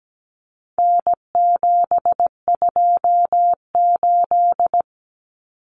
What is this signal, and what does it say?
Morse code, heard as a keyed tone, about 17 wpm, 700 Hz. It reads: N728
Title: Common Cut Numbers - 4-Digit 17wpm